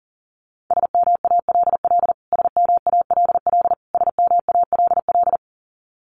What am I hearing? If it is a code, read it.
Morse code, heard as a keyed tone, about 40 wpm, 700 Hz. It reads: SMALL SMALL SMALL
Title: Top 500 Words Repeated 40wpm